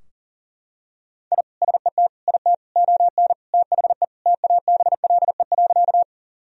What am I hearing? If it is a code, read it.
Morse code, heard as a keyed tone, about 40 wpm, 700 Hz. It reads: I SET IT ON THE TABLE.
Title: Sentences from Top 1000 Words - Review Part 2 - 40wpm